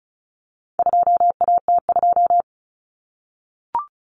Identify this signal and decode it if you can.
Morse code, heard as a keyed tone, about 34 wpm, 700 Hz. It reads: 2AT2
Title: Common Cut Numbers - 4-Digit 35wpm